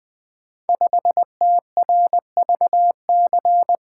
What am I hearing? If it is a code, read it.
Morse code, heard as a keyed tone, about 20 wpm, 700 Hz. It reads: 5TRVC
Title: Code Groups - 5 Letter:Numbers - 20wpm